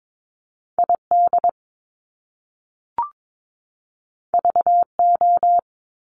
Morse code, heard as a keyed tone, about 22 wpm, 700 Hz. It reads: ID VO
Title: Top Two Letter Combinations in Words 22wpm